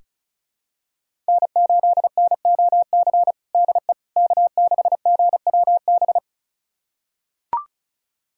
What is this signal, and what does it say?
Morse code, heard as a keyed tone, about 36 wpm, 700 Hz. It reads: N8NOC DE K6GWB
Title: USA Callsign DE Callsign - 5 Characters - 35wpm